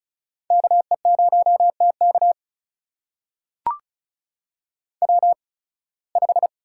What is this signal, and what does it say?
Morse code, heard as a keyed tone, about 36 wpm, spaced as for 23 wpm, 700 Hz. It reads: KE0TK W5
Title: USA Call Signs - 5 Characters - 8x Character Spacing - 35wpm